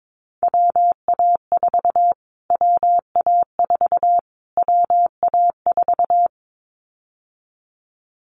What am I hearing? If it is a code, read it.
Morse code, heard as a keyed tone, about 22 wpm, 700 Hz. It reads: WA4 WA4 WA4
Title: Callsign Prefixes for Top 25 DXCC - Repeated 3x - 22wpm